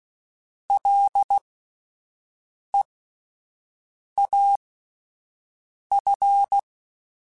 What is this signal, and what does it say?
Morse code, heard as a keyed tone, about 16 wpm, spaced as for 5 wpm, 800 Hz. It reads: LEAF